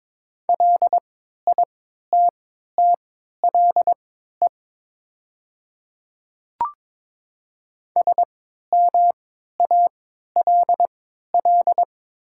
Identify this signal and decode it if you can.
Morse code, heard as a keyed tone, about 22 wpm, spaced as for 12 wpm, 700 Hz. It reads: LITTLE SMALL
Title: 3x Character Spacing - Top 200 Words - 22wpm